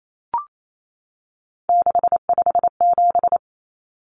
Morse code, heard as a keyed tone, about 28 wpm, 700 Hz. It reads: 657
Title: Common Cut Numbers - 3-Digit 28wpm